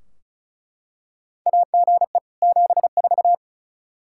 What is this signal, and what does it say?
Morse code, heard as a keyed tone, about 34 wpm, 700 Hz. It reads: AGE 74
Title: Practice QSO - Exchange 3 - More Info - 35wpm